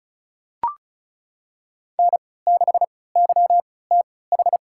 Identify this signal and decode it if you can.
Morse code, heard as a keyed tone, about 34 wpm, spaced as for 20 wpm, 700 Hz. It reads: N6YTH
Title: USA Call Signs - 5 Characters - 3x Character Spacing - 35wpm